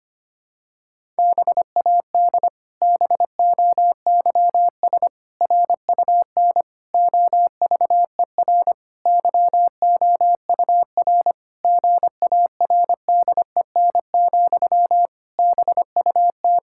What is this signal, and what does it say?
Morse code, heard as a keyed tone, about 25 wpm, 700 Hz. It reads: BAD BOYS RUN OVER YOUR GARDEN, BUT
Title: Pangram Challenge v2 - 25wpm